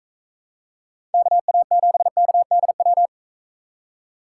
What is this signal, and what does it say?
Morse code, heard as a keyed tone, about 43 wpm, 700 Hz. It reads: KA7KDW